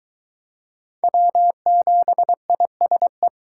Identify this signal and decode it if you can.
Morse code, heard as a keyed tone, about 23 wpm, 700 Hz. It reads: W7ISE